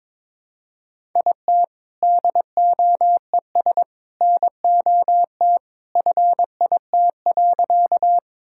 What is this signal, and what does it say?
Morse code, heard as a keyed tone, about 22 wpm, 700 Hz. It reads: IT DOES NOT FIT.
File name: practice / 22wpm / Sentences with 2-Word Phrases - 22wpm